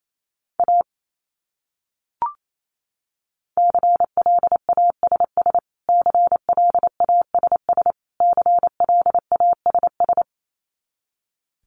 Morse code, heard as a keyed tone, about 28 wpm, 700 Hz. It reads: A CLASS CLASS CLASS
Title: Top 500 Words Repeated 28wpm